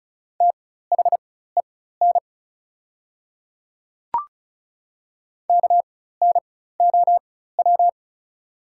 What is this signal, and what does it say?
Morse code, heard as a keyed tone, about 36 wpm, spaced as for 16 wpm, 690 Hz. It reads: THEN KNOW
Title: 4x Character Spacing - Top 100 Words - 35wpm